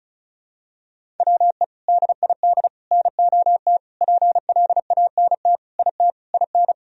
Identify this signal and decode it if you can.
Morse code, heard as a keyed tone, about 36 wpm, 700 Hz. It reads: WE DID NOT PLANT IT IN
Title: Sentences from Top 1000 Words - Review Part 1 - 35wpm